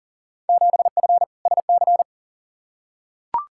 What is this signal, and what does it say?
Morse code, heard as a keyed tone, about 40 wpm, 700 Hz. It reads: 7F SC
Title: Field Day Exchanges Encore - 40wpm